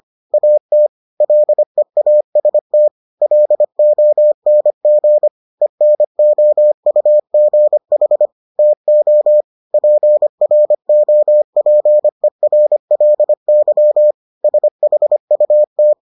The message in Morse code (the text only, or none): AT LEAST LONG ENOUGH TO PROPERLY SHUT